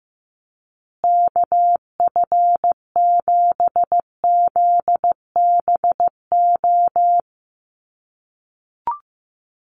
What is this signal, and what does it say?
Morse code, heard as a keyed tone, about 15 wpm, 700 Hz. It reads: KF7ZBO
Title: USA Call Signs Repeated 3x 15wpm